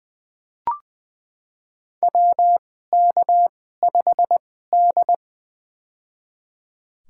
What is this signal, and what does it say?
Morse code, heard as a keyed tone, about 20 wpm, spaced as for 14 wpm, 700 Hz. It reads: WK5D
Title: USA Call Signs - 2x Character Spacing - 20wpm